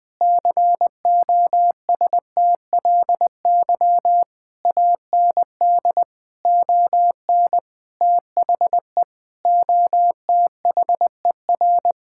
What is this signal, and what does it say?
Morse code, heard as a keyed tone, about 20 wpm, 700 Hz. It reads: COSTLY AND ON THE OTHER
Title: Sentences with 8-Word Phrases - Spoken Beginning - 20wpm